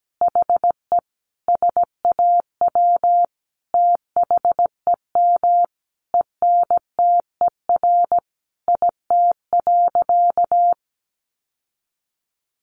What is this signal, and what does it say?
Morse code, heard as a keyed tone, about 17 wpm, 700 Hz. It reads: HE SAW THEM ENTER IT.
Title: Sentences from Top 1300 Words 17wpm